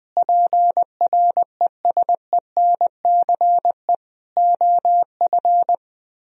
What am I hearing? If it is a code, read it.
Morse code, heard as a keyed tone, about 20 wpm, 700 Hz. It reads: PRESENCE OF
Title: Speed-Racing - Sets of 3 Words 17wpm